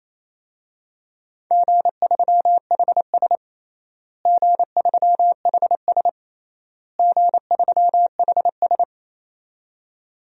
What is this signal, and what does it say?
Morse code, heard as a keyed tone, about 28 wpm, 700 Hz. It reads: G3HS G3HS G3HS
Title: DXCC-100 Call Signs - 4 Characters - Repeated 3x - 28wpm